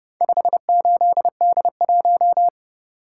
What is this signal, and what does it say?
Morse code, heard as a keyed tone, about 30 wpm, 700 Hz. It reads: 58D1